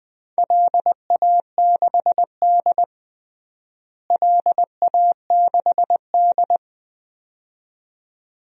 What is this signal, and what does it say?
Morse code, heard as a keyed tone, about 21 wpm, 700 Hz. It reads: LA6D LA6D
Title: DXCC-100 Call Signs - 4 Characters - Repeated 2x - 20wpm